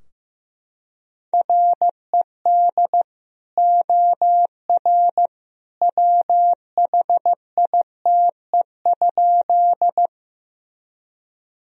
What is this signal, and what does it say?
Morse code, heard as a keyed tone, about 16 wpm, 700 Hz. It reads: RED OR WHITE?
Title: Sentences from Top 1000 Words - Review Part 2 - 15wpm